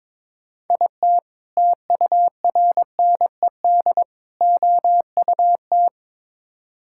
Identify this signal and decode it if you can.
Morse code, heard as a keyed tone, about 22 wpm, 700 Hz. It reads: IT TURNED OUT
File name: practice / 22wpm / Build A Phrase - 5-Word Phrases - 22wpm